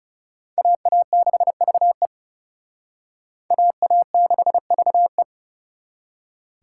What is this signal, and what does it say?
Morse code, heard as a keyed tone, about 35 wpm, 700 Hz. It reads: AA6VE AA6VE
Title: Speed Racing - USA Call Signs 30wpm